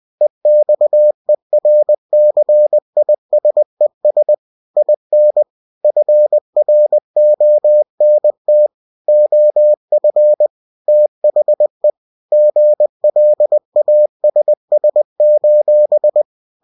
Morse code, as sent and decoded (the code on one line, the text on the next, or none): . -..- . .-. -.-. .. ... ... / .. -. / ..-. .-. --- -. - / --- ..-. / - .... / --. .-.. .- ... ... ---...
EXERCISES IN FRONT OF THE GLASS: